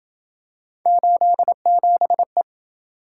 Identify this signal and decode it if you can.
Morse code, heard as a keyed tone, about 27 wpm, 700 Hz. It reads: 87E